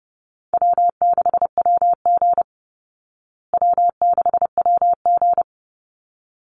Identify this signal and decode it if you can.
Morse code, heard as a keyed tone, about 30 wpm, 700 Hz. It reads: W6WG W6WG
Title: Call Signs Repeated 3x - 30wpm